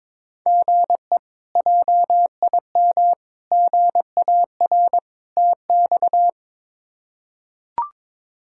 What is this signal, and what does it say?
Morse code, heard as a keyed tone, about 22 wpm, 700 Hz. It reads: GE JIM GAR TX
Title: K1USN SST QSO Practice - 22wpm